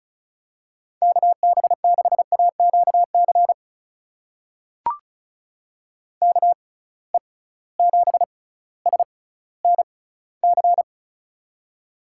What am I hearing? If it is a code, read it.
Morse code, heard as a keyed tone, about 36 wpm, spaced as for 18 wpm, 700 Hz. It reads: KB6AQC KE7SNC
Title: USA Call Signs - 6 Characters - 6x Character Spacing - 35wpm